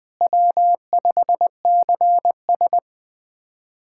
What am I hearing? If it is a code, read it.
Morse code, heard as a keyed tone, about 20 wpm, 700 Hz. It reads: W5CS
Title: USA Callsign DE Callsign - 4 Characters - 20wpm